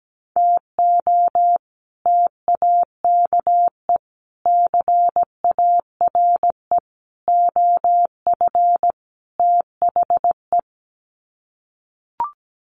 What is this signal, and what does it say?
Morse code, heard as a keyed tone, about 17 wpm, 700 Hz. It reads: TO TAKE CARE OF THE
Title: Sets of 5 Words Encore 17wpm